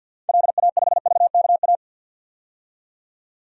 Morse code, heard as a keyed tone, about 50 wpm, 700 Hz. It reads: LA5VKA
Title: DXCC-100 Call Signs - 6 Characters - 50wpm